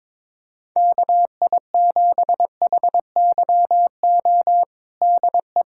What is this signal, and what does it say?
Morse code, heard as a keyed tone, about 22 wpm, 700 Hz. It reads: KI7HYO DE
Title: USA Callsign DE Callsign - 6 Characters - 22wpm